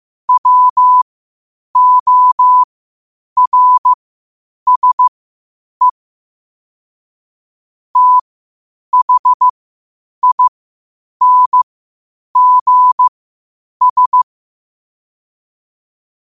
Morse code, heard as a keyed tone, about 15 wpm, spaced as for 8 wpm, 1000 Hz. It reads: WORSE THINGS
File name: qotdfiles / 8wpm